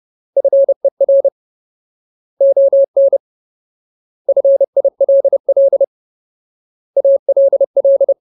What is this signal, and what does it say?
Morse code, heard as a keyed tone, about 30 wpm, 550 Hz. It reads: FER ON FILL ALL
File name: Ultimate Word List / 30WPM Ultimate